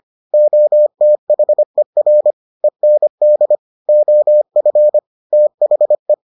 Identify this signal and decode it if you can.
Morse code, heard as a keyed tone, about 25 wpm, 600 Hz. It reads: OTHER END OF THE